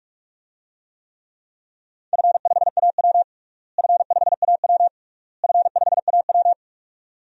Morse code, heard as a keyed tone, about 46 wpm, 700 Hz. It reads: F5AW F5AW F5AW
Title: DXCC-100 Call Signs - 4 Characters - Repeated 3x - 45wpm